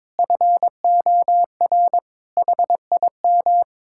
Morse code, heard as a keyed tone, about 22 wpm, 700 Hz. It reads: FOR HIM
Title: Sentences from Top 1600 Words 22wpm